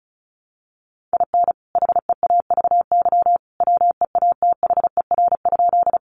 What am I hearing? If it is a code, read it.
Morse code, heard as a keyed tone, about 34 wpm, 700 Hz. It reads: IN HEAVY WEATHER?
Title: Sentences from Top 1100 Words 35wpm